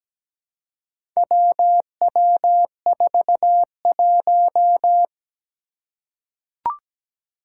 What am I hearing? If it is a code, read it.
Morse code, heard as a keyed tone, about 17 wpm, 700 Hz. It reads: WW41
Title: Four Challenging Letter-Number Combinations 17wpm